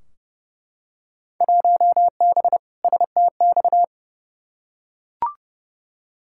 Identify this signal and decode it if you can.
Morse code, heard as a keyed tone, about 30 wpm, 700 Hz. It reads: 1B STX
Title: Field Day Exchanges Encore - 30wpm